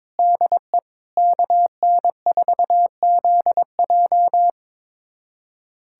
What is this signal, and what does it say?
Morse code, heard as a keyed tone, about 21 wpm, 700 Hz. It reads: DE KN4ZJ
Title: USA Callsign DE Callsign - 5 Characters - 22wpm